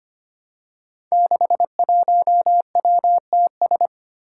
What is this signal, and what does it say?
Morse code, heard as a keyed tone, about 25 wpm, 700 Hz. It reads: 61WTS
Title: Five Challenging Letter-Number Combinations 25wpm